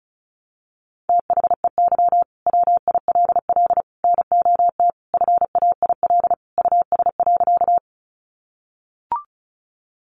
Morse code, heard as a keyed tone, about 34 wpm, 700 Hz. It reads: THEY WILL NOT FAIL US.
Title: Sentences from Top 1100 Words 35wpm